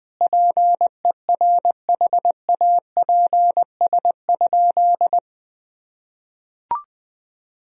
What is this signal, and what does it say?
Morse code, heard as a keyed tone, about 20 wpm, 700 Hz. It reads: PERHAPS?
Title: Sentences from Top 1000 Words - Review Part 2 - 20wpm